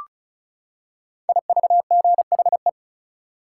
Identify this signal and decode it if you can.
Morse code, heard as a keyed tone, about 36 wpm, 700 Hz. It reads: IVGHE
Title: Code Groups - 5 Letters - 35wpm